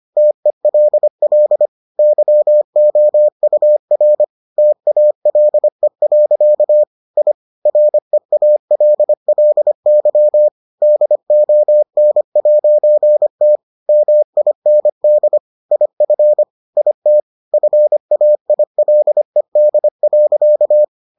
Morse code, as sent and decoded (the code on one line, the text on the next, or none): - . .-.. .-.. / -.-- --- ..- .-. / - .- .-.. .-.-.- / .. / .-. .- .-.. .-.. -.-- / -.. --- -. .----. - / -- .. -. -.. / .. ..-. / .. - / ..-. .- .. .-.. -.. .-.-.-
TELL YOUR TALE. I REALLY DON'T MIND IF IT FAILED.